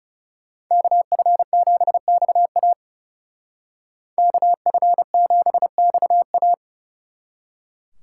Morse code, heard as a keyed tone, about 36 wpm, 700 Hz. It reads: KF7XA KF7XA
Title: Speed Racing - USA 2x2 Call Signs 30wpm